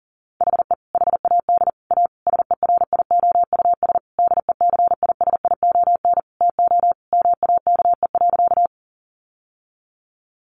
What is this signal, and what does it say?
Morse code, heard as a keyed tone, about 40 wpm, 700 Hz. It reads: HE HAD A SERIOUS DECISION TO MAKE.
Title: Sentences from Top 1100 Words 40wpm